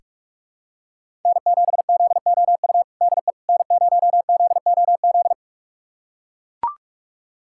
Morse code, heard as a keyed tone, about 43 wpm, 700 Hz. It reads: N7ZOU DE N0ZOZ
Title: USA Callsign DE Callsign - 5 Characters - 45wpm